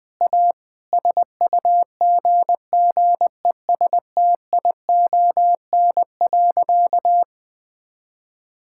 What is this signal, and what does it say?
Morse code, heard as a keyed tone, about 20 wpm, 700 Hz. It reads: A SUGGESTION.
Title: Sentences from Top 1600 Words 20wpm